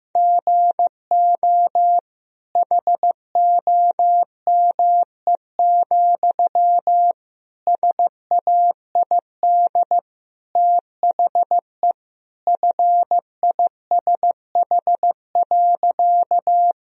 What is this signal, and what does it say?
Morse code, heard as a keyed tone, about 15 wpm, 700 Hz. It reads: GO HOME, SAID THE FISH.
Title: Sentences from Top 1000 Words - Review Part 1 - 15wpm